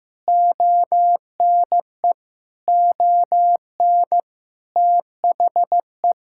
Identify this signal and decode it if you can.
Morse code, heard as a keyed tone, about 16 wpm, 700 Hz. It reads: ONE ON THE
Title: Sentences from Top 1000 Words - Review Part 6 - 15wpm